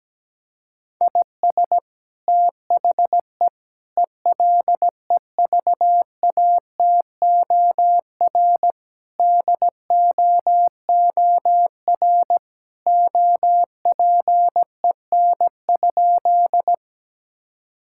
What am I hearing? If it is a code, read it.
Morse code, heard as a keyed tone, about 17 wpm, 700 Hz. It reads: IS THE ELEVATOR DOOR OPEN?
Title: Sentences from Top 1600 Words 17wpm